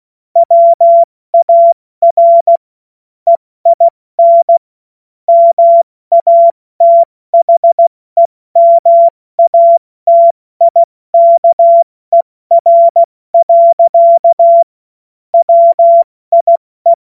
Morse code, heard as a keyed tone, about 16 wpm, 680 Hz. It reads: WAR EIN MATHEMATIKER. WIE
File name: Morsekurs / Witze / Witz 03. Mai 2022 70 F80